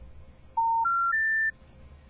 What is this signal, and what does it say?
Special information tone.